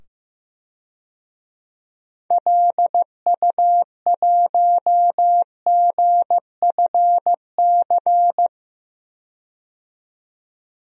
Morse code, heard as a keyed tone, about 15 wpm, 700 Hz. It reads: LU1GFC
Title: DXCC-100 Call Signs - 6 Characters - 15wpm